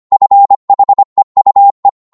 Morse code, heard as a keyed tone, about 25 wpm, 800 Hz. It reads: FHEUE